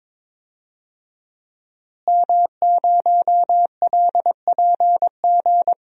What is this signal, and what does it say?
Morse code, heard as a keyed tone, about 22 wpm, 700 Hz. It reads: M0LPG
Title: DXCC-100 Call Signs - 5 Characters - Repeated 2x - 22wpm